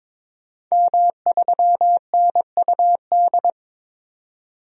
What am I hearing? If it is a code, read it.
Morse code, heard as a keyed tone, about 23 wpm, 700 Hz. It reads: M3NUD
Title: DXCC-100 Call Signs - 5 Characters - 22wpm